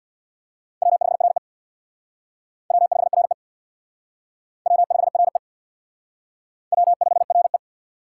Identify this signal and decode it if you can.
Morse code, heard as a keyed tone, about 64 wpm, 700 Hz. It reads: W5RE W5RE W5RE W5RE